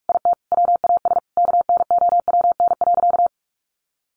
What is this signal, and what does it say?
Morse code, heard as a keyed tone, about 44 wpm, 700 Hz. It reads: IT WAS KNOWN.
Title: Sentences from Top 1400 Words 45wpm